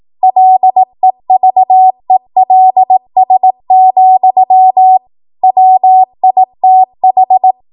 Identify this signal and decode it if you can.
Morse code, heard as a keyed tone, about 18 wpm, 750 Hz. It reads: LEVELS, WITH
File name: Morse / 18WPM